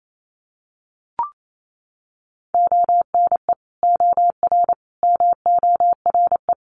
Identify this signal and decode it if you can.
Morse code, heard as a keyed tone, about 28 wpm, 690 Hz. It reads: ONE OR MORE
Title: Sets of 5 Words Encore 28wpm